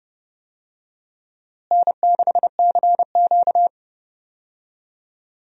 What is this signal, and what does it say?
Morse code, heard as a keyed tone, about 30 wpm, 700 Hz. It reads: N6CQ
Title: USA Call Signs - 4 Characters - Repeated 2x - 30wpm